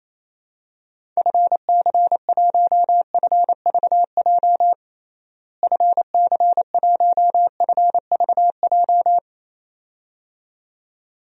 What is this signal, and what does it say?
Morse code, heard as a keyed tone, about 28 wpm, 700 Hz. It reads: FC1FVJ FC1FVJ
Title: DXCC-100 Call Signs - 6 Characters - Repeated 2x - 28wpm